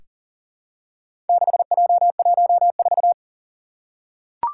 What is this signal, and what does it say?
Morse code, heard as a keyed tone, about 40 wpm, 700 Hz. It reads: 6J14
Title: Four Challenging Letter-Number Combinations 40wpm